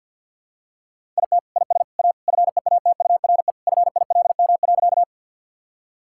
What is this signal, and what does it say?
Morse code, heard as a keyed tone, about 50 wpm, 700 Hz. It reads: IT IS A FEATURE FILM.
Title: Sentences from Top 1000 Words - Review Part 5 - 50wpm